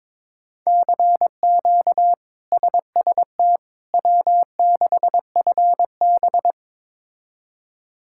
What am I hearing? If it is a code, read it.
Morse code, heard as a keyed tone, about 22 wpm, 700 Hz. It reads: CQ SST W6FB
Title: K1USN SST QSO Practice - 22wpm